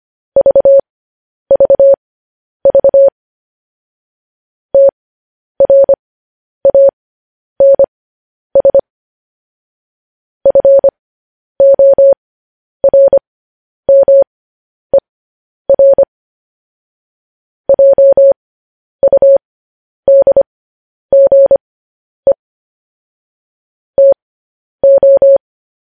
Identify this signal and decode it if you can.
Morse code, heard as a keyed tone, about 25 wpm, spaced as for 10 wpm, 550 Hz. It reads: VVV TRANS FORMER JUDGE TO
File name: podcast / News headlines-1x-25x10